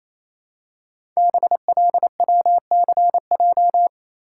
Morse code, heard as a keyed tone, about 27 wpm, 700 Hz. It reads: BLWCJ
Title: Code Groups - 5 Letters - 28wpm